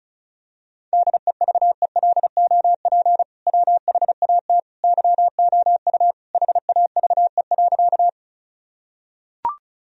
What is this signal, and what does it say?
Morse code, heard as a keyed tone, about 34 wpm, 700 Hz. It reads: DEVELOP WHAT YOU HAVE.
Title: Sentences from Top 1000 Words - Review Part 2 - 35wpm